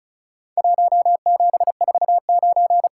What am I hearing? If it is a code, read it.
Morse code, heard as a keyed tone, about 36 wpm, 700 Hz. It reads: 1749
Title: Worked All Europe DX Contest - QTC Exchange - 35wpm